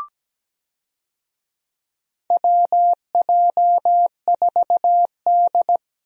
Morse code, heard as a keyed tone, about 17 wpm, 700 Hz. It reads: WJ4D
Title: USA Call Signs - 4 Characters - Repeated 3x - 17wpm